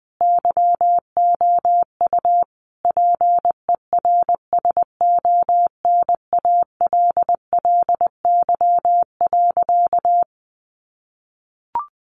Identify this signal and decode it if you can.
Morse code, heard as a keyed tone, about 20 wpm, 700 Hz. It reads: YOU PERSONALLY.
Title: Sentences from Top 1400 Words 20wpm